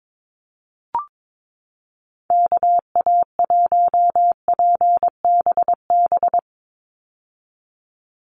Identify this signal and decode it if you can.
Morse code, heard as a keyed tone, about 22 wpm, 700 Hz. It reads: KA1PBB
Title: USA Call Signs 22wpm